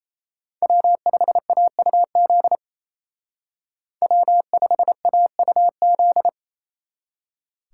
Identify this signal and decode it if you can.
Morse code, heard as a keyed tone, about 36 wpm, 700 Hz. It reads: W5AUZ W5AUZ